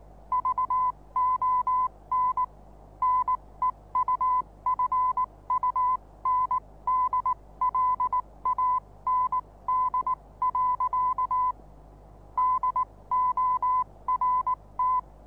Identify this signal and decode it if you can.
Morse code, heard as a keyed tone, about 19 wpm, spaced as for 16 wpm, 1010 Hz. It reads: VON NEUFUNDLAND. DORT
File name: repos / Morsekurs / DJ4UF Morsen26